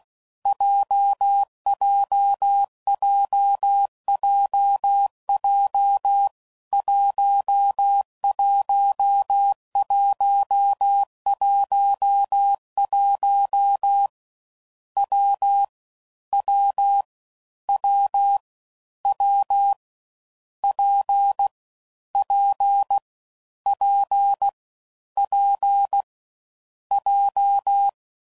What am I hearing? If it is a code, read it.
Morse code, heard as a keyed tone, about 16 wpm, 800 Hz. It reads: JJJJJ 11111 W W W W P P P P J
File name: Combinations / Difficult Combos / WPJ WPJ1